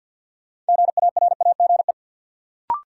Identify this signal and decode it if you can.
Morse code, heard as a keyed tone, about 52 wpm, 700 Hz. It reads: GARAGE
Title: Sentences from Top 1600 Words - New Word Drill - 50wpm